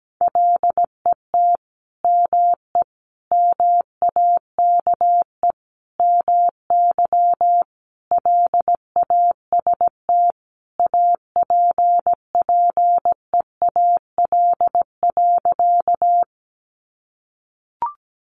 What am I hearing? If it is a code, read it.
Morse code, heard as a keyed tone, about 17 wpm, 700 Hz. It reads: LET ME MAKE MY LAST APPEAL.
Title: Sentences from Top 1300 Words 17wpm